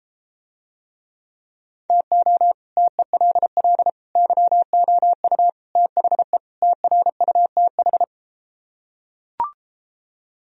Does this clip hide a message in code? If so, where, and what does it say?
Yes: Morse code from 1.9 to 8.0 s, heard as a keyed tone, about 33 wpm, 700 Hz. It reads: TO TELL YOU THE TRUTH